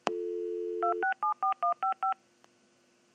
DTMF, heard as telephone touch tones, tones of about 120 ms. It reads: 2674155
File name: dial-up-modem-1